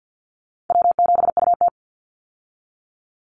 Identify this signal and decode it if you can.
Morse code, heard as a keyed tone, about 50 wpm, 700 Hz. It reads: W7UT
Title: Call Signs Repeated 2x - 50wpm